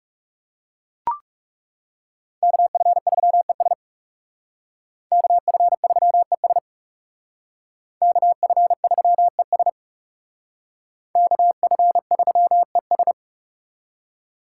Morse code, heard as a keyed tone, about 39 wpm, 700 Hz. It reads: KF3ES KF3ES KF3ES KF3ES